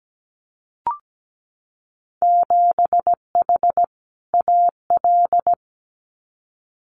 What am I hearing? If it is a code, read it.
Morse code, heard as a keyed tone, about 17 wpm, 700 Hz. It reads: 7H AL
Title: Winter Field Day Exchanges Encore 17wpm